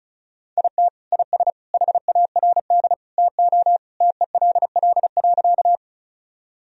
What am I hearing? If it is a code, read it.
Morse code, heard as a keyed tone, about 36 wpm, 700 Hz. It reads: IT IS HARD TO TELL.
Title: Sentences from Top 1000 Words - Review Part 1 - 35wpm